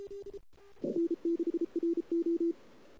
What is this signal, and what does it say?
Morse code, heard as a keyed tone, about 33 wpm, 340 Hz. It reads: N6RO
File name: CW / 2008-ss-cw-k5tr-op-n5rz -02:00